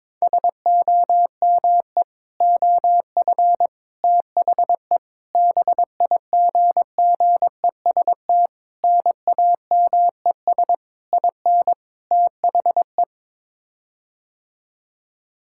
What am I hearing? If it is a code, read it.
Morse code, heard as a keyed tone, about 22 wpm, 700 Hz. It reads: SOME OF THE BIGGEST NAMES IN THE
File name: practice / 22wpm / Sets of 7 Words - 22wpm